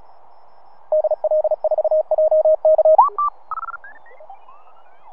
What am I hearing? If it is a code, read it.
Morse code, heard as a keyed tone, about 36 wpm, 620 Hz. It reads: DL4JK